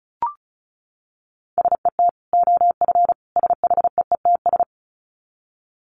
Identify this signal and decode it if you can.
Morse code, heard as a keyed tone, about 34 wpm, 700 Hz. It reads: SET OF SHEETS
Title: Dit Challenge 35wpm